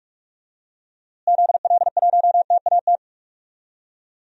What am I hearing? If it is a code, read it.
Morse code, heard as a keyed tone, about 41 wpm, 700 Hz. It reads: ZL1TAT
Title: DXCC-100 Call Signs - 6 Characters - Repeated 3x - 45wpm